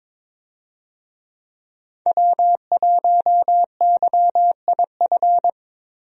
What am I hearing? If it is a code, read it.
Morse code, heard as a keyed tone, about 22 wpm, 700 Hz. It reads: W1YIF
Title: USA Call Signs - 5 Characters - 22wpm